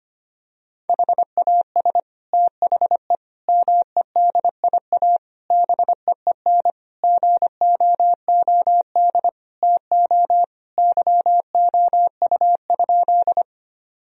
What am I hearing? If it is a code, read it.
Morse code, heard as a keyed tone, about 25 wpm, 700 Hz. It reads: HAS THE MEDIA BEEN GOOD TO YOU?
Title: Sentences from Top 1000 Words - Review Part 4 - 25wpm